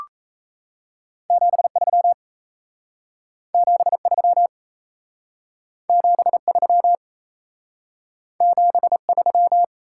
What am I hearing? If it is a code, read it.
Morse code, heard as a keyed tone, about 38 wpm, 700 Hz. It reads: 73 73 73 73